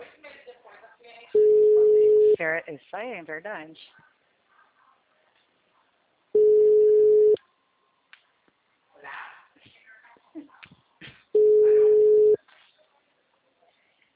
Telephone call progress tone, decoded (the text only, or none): ringback tone